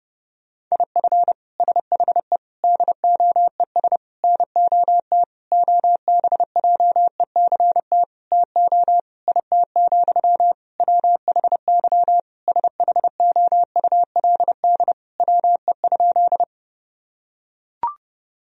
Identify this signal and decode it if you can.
Morse code, heard as a keyed tone, about 32 wpm, 700 Hz. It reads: IF SHE DOES NOT OBJECT TO IT, WHY SHOULD WE?
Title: Sentences from Top 1000 Words - Review Part 3 - 30wpm